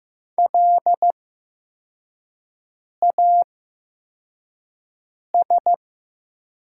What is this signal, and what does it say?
Morse code, heard as a keyed tone, about 15 wpm, spaced as for 4 wpm, 700 Hz. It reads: LAS
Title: 8x Character Spacing - Top 300 Words - 15wpm